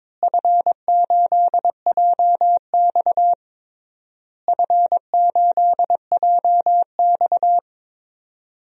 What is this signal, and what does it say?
Morse code, heard as a keyed tone, about 21 wpm, 700 Hz. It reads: F8JX F8JX
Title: DXCC-100 Call Signs - 4 Characters - Repeated 3x - 22wpm